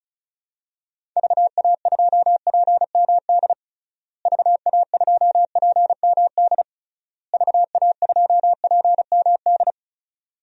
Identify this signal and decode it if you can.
Morse code, heard as a keyed tone, about 34 wpm, 700 Hz. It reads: VA2PMD VA2PMD VA2PMD
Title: DXCC-100 Call Signs - 6 Characters - Repeated 3x - 35wpm